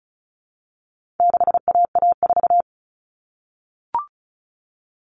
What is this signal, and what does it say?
Morse code, heard as a keyed tone, about 34 wpm, 700 Hz. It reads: 6AA4
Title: All Cut Numbers - 4-Digit 35wpm